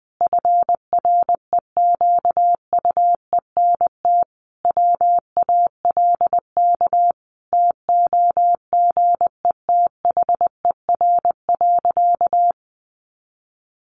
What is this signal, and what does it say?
Morse code, heard as a keyed tone, about 20 wpm, 700 Hz. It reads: FREQUENT WALK TOGETHER.
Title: Sentences from Top 1400 Words 20wpm